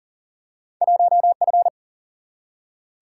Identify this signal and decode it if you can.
Morse code, heard as a keyed tone, about 40 wpm, 700 Hz. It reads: 1F
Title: Single-Letter Single-Number 40wpm